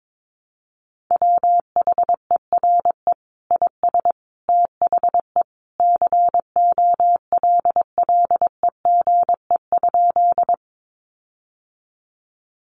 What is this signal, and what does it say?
Morse code, heard as a keyed tone, about 22 wpm, 700 Hz. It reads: WHERE IS THE COLLEGE?
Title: Sentences from Top 1300 Words 22wpm